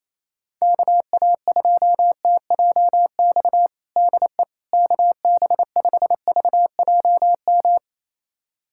Morse code, heard as a keyed tone, about 28 wpm, 700 Hz. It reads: KA2TJX DE KB5VJM
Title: USA Callsign DE Callsign 28wpm